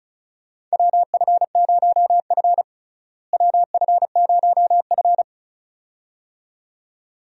Morse code, heard as a keyed tone, about 36 wpm, 700 Hz. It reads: WF0F WF0F
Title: USA Call Signs - 4 Characters - Repeated 2x - 35wpm